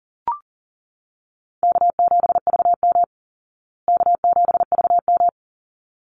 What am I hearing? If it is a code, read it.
Morse code, heard as a keyed tone, about 40 wpm, 700 Hz. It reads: K7VM K7VM
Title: Easy Call Signs Repeated 2x - 40wpm